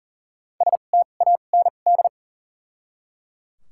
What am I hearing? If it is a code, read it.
Morse code, heard as a keyed tone, about 40 wpm, spaced as for 29 wpm, 700 Hz. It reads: STAND
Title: 2x Character Spacing - Top 300 Words - 40wpm